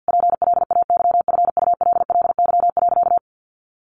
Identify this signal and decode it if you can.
Morse code, heard as a keyed tone, about 50 wpm, 700 Hz. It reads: PLAYFULLY.
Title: A Sound Of Thunder 50wpm